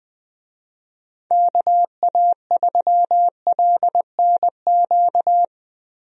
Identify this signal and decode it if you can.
Morse code, heard as a keyed tone, about 20 wpm, 700 Hz. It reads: KA3LNQ
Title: USA Call Signs - 6 Characters - Repeated 2x - 20wpm